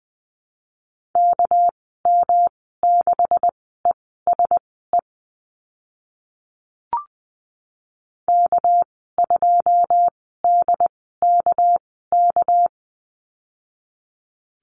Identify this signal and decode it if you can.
Morse code, heard as a keyed tone, about 20 wpm, spaced as for 14 wpm, 700 Hz. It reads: KM6ESE K2DKK
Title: 213 Lesson 34g - Number 8 - Call Signs 20wpm